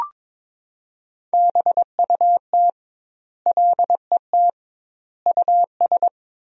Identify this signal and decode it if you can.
Morse code, heard as a keyed tone, about 21 wpm, 700 Hz. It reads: BUT LET US
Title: 2x Word-Spacing - Sentences from Top 500 Words - 22wpm